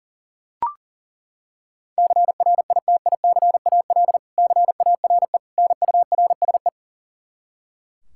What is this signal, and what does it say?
Morse code, heard as a keyed tone, about 40 wpm, 700 Hz. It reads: CRITICAL CARE NURSE
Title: Practice QSO - Exchange 3 - More Info - 40wpm